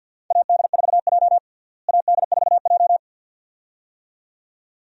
Morse code, heard as a keyed tone, about 53 wpm, 700 Hz. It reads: AD4J AD4J
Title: USA Call Signs - 4 Characters - Repeated 2x - 50wpm